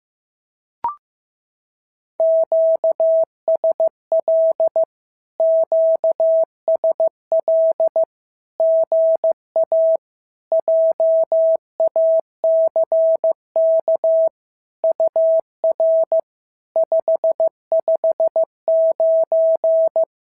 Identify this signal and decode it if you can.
Morse code, heard as a keyed tone, about 15 wpm, 650 Hz. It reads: QSL QSL GA JACK UR 559
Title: SOTA QSO Practice 15wpm